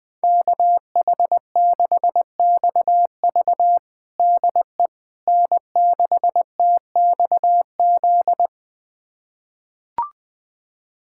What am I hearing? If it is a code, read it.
Morse code, heard as a keyed tone, about 20 wpm, 700 Hz. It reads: KH6XV DE N6TXZ
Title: USA Callsign DE Callsign - 5 Characters - 20wpm